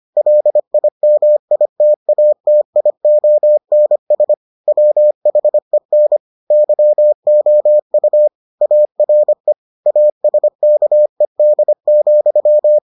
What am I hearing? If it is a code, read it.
Morse code, heard as a keyed tone, about 25 wpm, 600 Hz. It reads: LIMITATIONS WHEN YOU ARE ASKED,